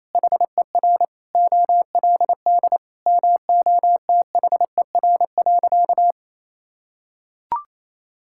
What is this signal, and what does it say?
Morse code, heard as a keyed tone, about 28 wpm, 700 Hz. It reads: HER OLD MOTHER.
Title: Sentences from Top 1000 Words - Review Part 5 - 28wpm